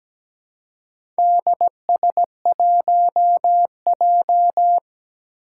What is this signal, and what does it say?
Morse code, heard as a keyed tone, about 17 wpm, 700 Hz. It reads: DS1J
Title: Four Challenging Letter-Number Combinations 17wpm